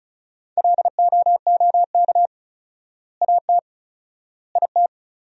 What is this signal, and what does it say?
Morse code, heard as a keyed tone, about 34 wpm, 700 Hz. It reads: LOOK AT IT
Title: 4x Word-Spacing - Sentences from Top 200 Words - 35wpm